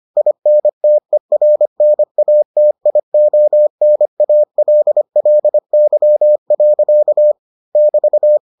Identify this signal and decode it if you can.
Morse code, heard as a keyed tone, about 25 wpm, 600 Hz. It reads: INTERNATIONALLY. <BT>